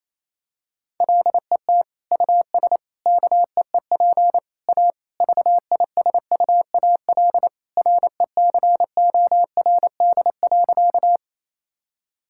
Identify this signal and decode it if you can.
Morse code, heard as a keyed tone, about 28 wpm, 700 Hz. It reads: LET US KEEP A VISUAL RECORD.
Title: Sentences from Top 1000 Words - Review Part 6 - 28wpm